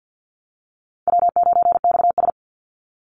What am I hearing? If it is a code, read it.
Morse code, heard as a keyed tone, about 50 wpm, 700 Hz. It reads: W9XS
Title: Call Signs Repeated 3x - 50wpm